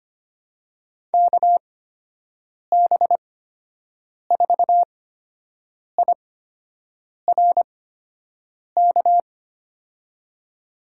Morse code, heard as a keyed tone, about 25 wpm, spaced as for 7 wpm, 700 Hz. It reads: KB4IRK
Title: USA Call Signs - 8x Character Spacing - 25wpm